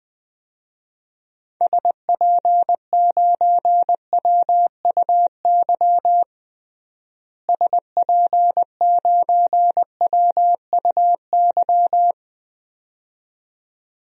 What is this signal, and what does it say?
Morse code, heard as a keyed tone, about 20 wpm, 690 Hz. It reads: SP9WUY SP9WUY
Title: DXCC-100 Call Signs - 6 Characters - Repeated 2x - 20wpm